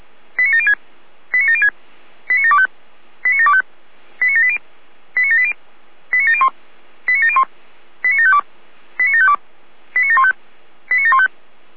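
Five-tone selective call (ZVEI), heard as quick runs of five tones, tones of about 70 ms. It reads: 89706 89706 89725 89725 89780 89780 89701 89701 89752 89752 89715 89715
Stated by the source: ZVEI five-tone selective call, address 89432, received on an SDR.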